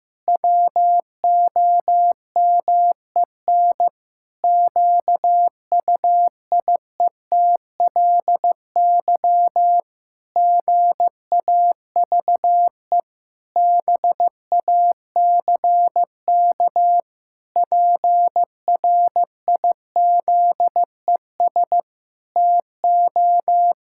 Morse code, heard as a keyed tone, about 15 wpm, 700 Hz. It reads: WOMEN QUIETLY GAVE BACK PRIZES TO
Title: Pangram Challenge v2 - 15wpm